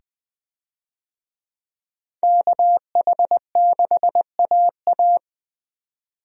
Morse code, heard as a keyed tone, about 21 wpm, 700 Hz. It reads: KH6AA
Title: DXCC-100 Call Signs - 5 Characters - 20wpm